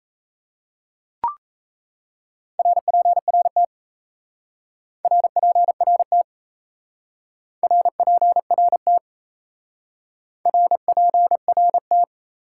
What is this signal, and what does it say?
Morse code, heard as a keyed tone, about 34 wpm, 700 Hz. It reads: RPRT RPRT RPRT RPRT
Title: Speed-Racing Top 180 QSO Elements 28wpm